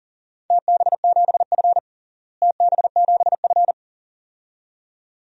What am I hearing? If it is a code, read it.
Morse code, heard as a keyed tone, about 40 wpm, 700 Hz. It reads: TB7F TB7F
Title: DXCC-100 Call Signs - 4 Characters - Repeated 2x - 40wpm